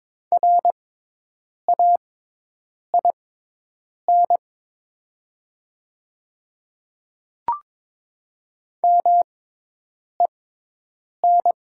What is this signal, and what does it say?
Morse code, heard as a keyed tone, about 22 wpm, spaced as for 8 wpm, 700 Hz. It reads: RAIN MEN
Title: 6x Character Spacing - Top 300 Words - 22wpm